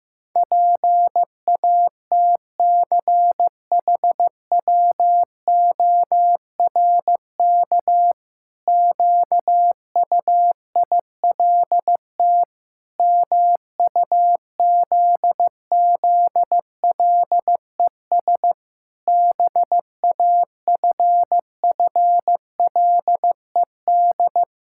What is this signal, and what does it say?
Morse code, heard as a keyed tone, about 15 wpm, 700 Hz. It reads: PATCHWORK QUILT MUZZLES BAFFLED
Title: Pangram Challenge v2 - 15wpm